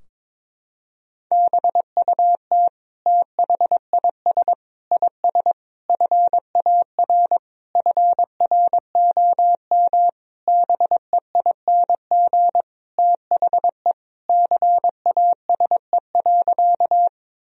Morse code, heard as a keyed tone, about 21 wpm, 700 Hz. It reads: BUT THIS IS FAR FROM BEING THE CASE.
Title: Sentences from Top 1000 Words - Review Part 3 - 22wpm